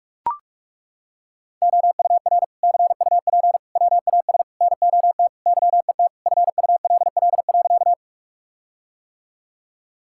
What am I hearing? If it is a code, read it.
Morse code, heard as a keyed tone, about 43 wpm, 700 Hz. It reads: OUR CUP WAS NOT YET FULL.
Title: Sentences from Top 1600 Words 45wpm